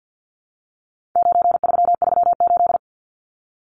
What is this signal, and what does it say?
Morse code, heard as a keyed tone, about 49 wpm, 700 Hz. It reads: 9338
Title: Numbers - 4-Digit 50wpm